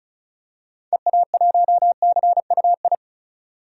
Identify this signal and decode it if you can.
Morse code, heard as a keyed tone, about 36 wpm, 700 Hz. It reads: EA1CUI
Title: DXCC-100 Call Signs - 6 Characters - 35wpm